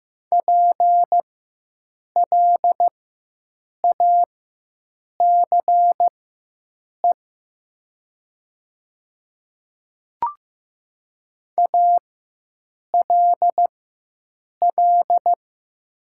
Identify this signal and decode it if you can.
Morse code, heard as a keyed tone, about 15 wpm, spaced as for 7 wpm, 700 Hz. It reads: PLACE ALL
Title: 4x Character Spacing - Top 200 Words - 15wpm